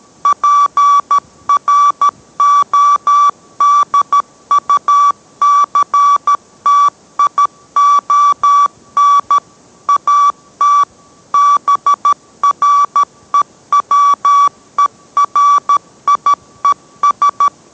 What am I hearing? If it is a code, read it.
Morse code, heard as a keyed tone, about 13 wpm, 1150 Hz. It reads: PRODUCTION AT BREWERIES